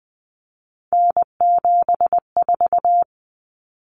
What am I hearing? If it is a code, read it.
Morse code, heard as a keyed tone, about 20 wpm, 700 Hz. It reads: N74
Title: Common Cut Numbers - 3-Digit 20wpm